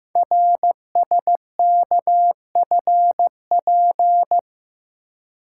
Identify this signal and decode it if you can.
Morse code, heard as a keyed tone, about 15 wpm, 700 Hz. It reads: RSKFP
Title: Code Groups - 5 Letters - 15wpm